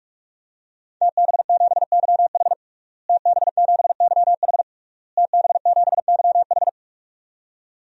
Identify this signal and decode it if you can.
Morse code, heard as a keyed tone, about 46 wpm, 700 Hz. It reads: TB7YH TB7YH TB7YH
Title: DXCC-100 Call Signs - 5 Characters - Repeated 3x - 45wpm